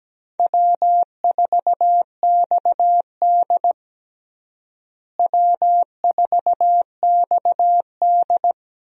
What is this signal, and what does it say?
Morse code, heard as a keyed tone, about 17 wpm, 700 Hz. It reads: W4XD W4XD
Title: USA Call Signs - 4 Characters - Repeated 3x - 17wpm